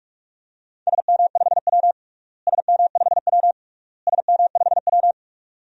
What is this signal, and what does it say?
Morse code, heard as a keyed tone, about 43 wpm, 700 Hz. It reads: SM5W SM5W SM5W
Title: DXCC-100 Call Signs - 4 Characters - Repeated 3x - 45wpm